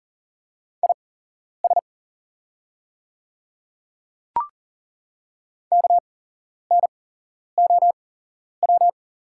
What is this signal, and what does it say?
Morse code, heard as a keyed tone, about 40 wpm, spaced as for 11 wpm, 700 Hz. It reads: IS KNOW
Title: 8x Character Spacing - Top 200 Words - 40wpm